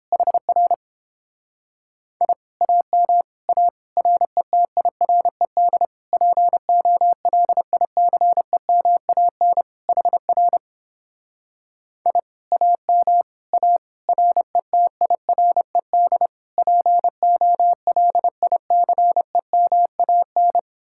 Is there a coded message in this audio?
Yes: Morse code from 0.1 to 20.6 s, heard as a keyed tone, about 29 wpm, 700 Hz. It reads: HR I AM A RETIRED POLICEMAN HR I AM A RETIRED POLICEMAN